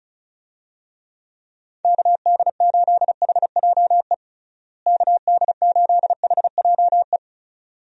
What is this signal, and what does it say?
Morse code, heard as a keyed tone, about 34 wpm, 700 Hz. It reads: KD8HJE KD8HJE
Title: USA Call Signs - 6 Characters - Repeated 2x - 35wpm